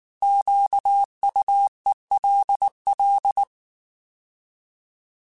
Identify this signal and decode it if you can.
Morse code, heard as a keyed tone, about 19 wpm, 800 Hz. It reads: QUELL
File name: Alphabetical / Alphabetical mp3s / Qs @18